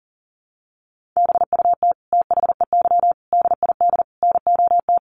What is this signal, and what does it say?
Morse code, heard as a keyed tone, about 40 wpm, 700 Hz. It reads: BUT THEY DID NOT
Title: Sentences from Top 1400 Words 40wpm